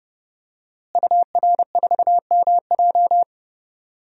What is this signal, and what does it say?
Morse code, heard as a keyed tone, about 30 wpm, 700 Hz. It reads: UR4MJ
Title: DXCC-100 Call Signs - 5 Characters - Repeated 2x - 30wpm